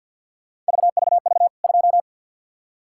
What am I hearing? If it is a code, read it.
Morse code, heard as a keyed tone, about 52 wpm, 700 Hz. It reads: VVV 2
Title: K1USN SST QSO Practice 50wpm